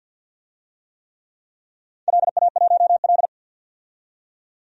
Morse code, heard as a keyed tone, about 51 wpm, 700 Hz. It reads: LA1L
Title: DXCC-100 Call Signs - 4 Characters - 50wpm